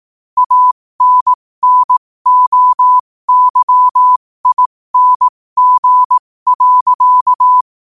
Morse code, heard as a keyed tone, about 18 wpm, spaced as for 15 wpm, 1000 Hz. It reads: ANNOYING.
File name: qotdfiles / 15wpm